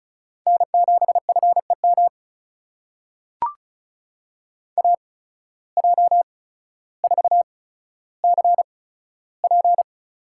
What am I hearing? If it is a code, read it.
Morse code, heard as a keyed tone, about 36 wpm, spaced as for 15 wpm, 700 Hz. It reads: N7FEM AJ4CP
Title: USA Call Signs - 5 Characters - 8x Character Spacing - 35wpm